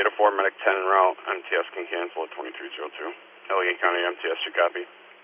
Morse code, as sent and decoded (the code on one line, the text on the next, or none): none